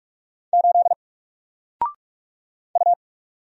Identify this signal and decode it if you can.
Morse code, heard as a keyed tone, about 44 wpm, 700 Hz. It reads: 8 U
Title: Single Letter-Number Alt 45wpm